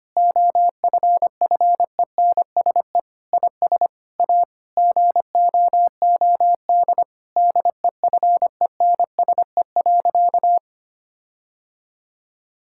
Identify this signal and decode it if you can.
Morse code, heard as a keyed tone, about 25 wpm, 700 Hz. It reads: OFFENSE IS A GOOD DEFENSE.
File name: practice / 25wpm / Difficult Sentences with Hints - 25wpm